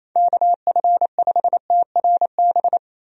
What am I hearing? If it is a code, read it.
Morse code, heard as a keyed tone, about 28 wpm, 700 Hz. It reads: KF5TRB
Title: USA Call Signs - 6 Characters - 28wpm